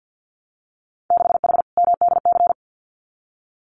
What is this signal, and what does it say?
Morse code, heard as a keyed tone, about 50 wpm, 700 Hz. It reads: 6H MDC
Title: Winter Field Day Exchanges Encore 50wpm